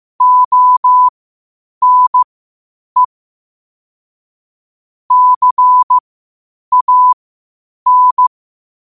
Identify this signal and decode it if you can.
Morse code, heard as a keyed tone, about 15 wpm, spaced as for 8 wpm, 1000 Hz. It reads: ONE CAN
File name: qotdfiles / 8wpm